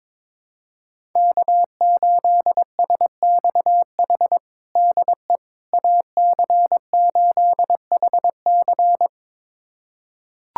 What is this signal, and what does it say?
Morse code, heard as a keyed tone, about 21 wpm, 700 Hz. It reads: K8SXH DE AC8HC
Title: USA Callsign DE Callsign 22wpm